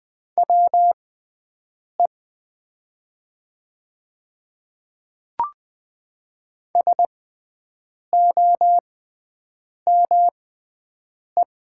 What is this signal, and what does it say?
Morse code, heard as a keyed tone, about 20 wpm, spaced as for 7 wpm, 690 Hz. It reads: WE SOME
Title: 6x Character Spacing - Top 100 Words - 20wpm